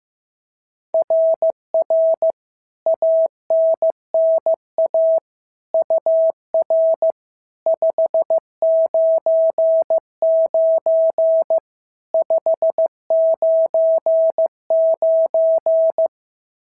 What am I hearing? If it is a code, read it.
Morse code, heard as a keyed tone, about 14 wpm, 650 Hz. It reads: RR ANNA UR 599 599
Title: SOTA QSO Practice 15wpm